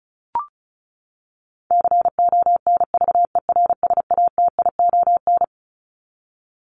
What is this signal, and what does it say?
Morse code, heard as a keyed tone, about 34 wpm, 700 Hz. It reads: CONVERSATION
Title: Top 1751-2000 Words 35wpm